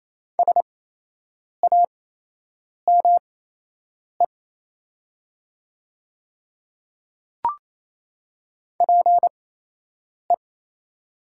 Morse code, heard as a keyed tone, about 28 wpm, spaced as for 8 wpm, 700 Hz. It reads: SAME PE
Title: 8x Character Spacing - Top 200 Words - 28wpm